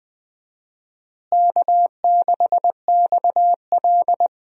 Morse code, heard as a keyed tone, about 20 wpm, 700 Hz. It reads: K6XL
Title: USA Call Signs - 4 Characters - Repeated 2x - 20wpm